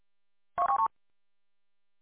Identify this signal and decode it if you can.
There is no code.